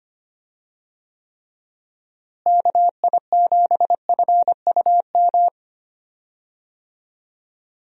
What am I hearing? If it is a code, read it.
Morse code, heard as a keyed tone, about 25 wpm, 700 Hz. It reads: KI7FUM
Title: USA Call Signs - 6 Characters - Repeated 2x - 25wpm